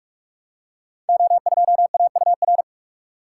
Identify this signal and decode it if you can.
Morse code, heard as a keyed tone, about 44 wpm, 700 Hz. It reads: O2AUR
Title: Code Groups - 5 Letter:Numbers - 45wpm